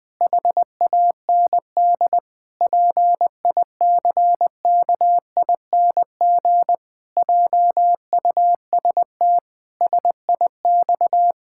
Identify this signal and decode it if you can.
Morse code, heard as a keyed tone, about 20 wpm, 700 Hz. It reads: HAND PICKING JUST SIX
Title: Pangram Challenge v2 - 20wpm